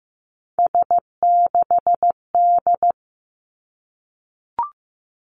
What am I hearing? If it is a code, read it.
Morse code, heard as a keyed tone, about 15 wpm, 700 Hz. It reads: S6D
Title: Three Challenging Letter-Number Combinations 15wpm